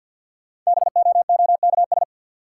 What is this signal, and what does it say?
Morse code, heard as a keyed tone, about 53 wpm, 700 Hz. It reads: BOOKS
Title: Sentences from Top 1200 Words - New Word Drill Alt - 50wpm